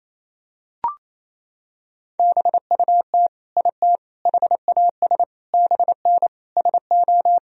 Morse code, heard as a keyed tone, about 28 wpm, 700 Hz. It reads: BUT IT HAS BN SO
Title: Sentences from Top 100 Words and Abbreviations 28wpm